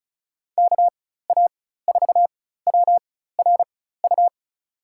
Morse code, heard as a keyed tone, about 34 wpm, spaced as for 16 wpm, 700 Hz. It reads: KA4WRU
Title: USA Call Signs - 6 Characters - 4x Character Spacing - 35wpm